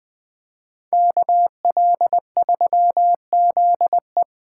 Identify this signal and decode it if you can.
Morse code, heard as a keyed tone, about 21 wpm, 700 Hz. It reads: KL3ZE